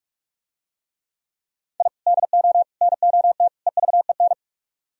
Morse code, heard as a keyed tone, about 46 wpm, 700 Hz. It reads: I DO NOT EVEN